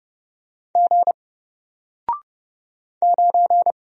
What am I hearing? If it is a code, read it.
Morse code, heard as a keyed tone, about 30 wpm, 700 Hz. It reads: G 9
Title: Single Character (Letter:Number:Punctuation:Prosign) 30wpm